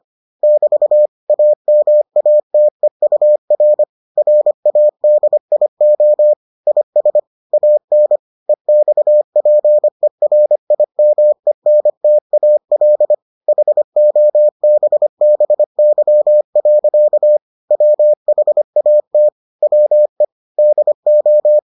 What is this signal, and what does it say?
Morse code, heard as a keyed tone, about 25 wpm, 600 Hz. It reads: <BT> AMATEUR RADIO IS AN EXPERIMENTAL HOBBY. WHAT WE DO